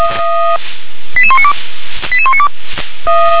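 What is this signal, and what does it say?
Five-tone selective call (ZVEI), heard as quick runs of five tones, tones of about 70 ms. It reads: 88192 88192
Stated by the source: ZVEI five-tone call 88158 received by SDR.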